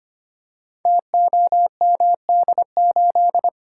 Morse code, heard as a keyed tone, about 25 wpm, 700 Hz. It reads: TOMD8
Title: Code Groups - 5 Letter:Numbers - 25wpm